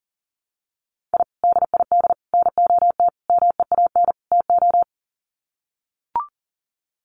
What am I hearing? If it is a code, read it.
Morse code, heard as a keyed tone, about 40 wpm, 700 Hz. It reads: I DID NOT MEAN TO
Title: Sets of 5 Words 40wpm